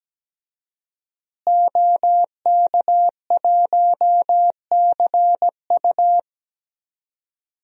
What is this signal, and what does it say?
Morse code, heard as a keyed tone, about 17 wpm, 700 Hz. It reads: OK1CU
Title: DXCC-100 Call Signs - 5 Characters - Repeated 2x - 17wpm